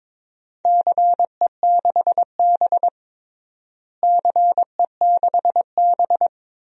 Morse code, heard as a keyed tone, about 22 wpm, 700 Hz. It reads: CE6B CE6B
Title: DXCC-100 Call Signs - 4 Characters - Repeated 2x - 22wpm